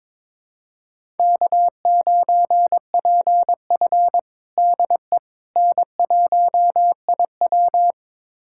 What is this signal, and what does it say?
Morse code, heard as a keyed tone, about 22 wpm, 700 Hz. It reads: K9PF DE N1IW
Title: USA Callsign DE Callsign - 4 Characters - 22wpm